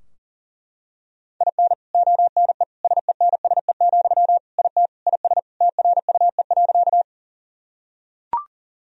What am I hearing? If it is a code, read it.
Morse code, heard as a keyed tone, about 40 wpm, 700 Hz. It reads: IN ONE SENSE, IT IS TRUE.
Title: Sentences from Top 1000 Words - Review Part 4 - 40wpm